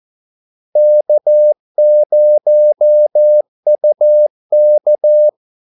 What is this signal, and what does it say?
Morse code, heard as a keyed tone, about 14 wpm, 600 Hz. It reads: K0UK
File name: Callsigns-14wpm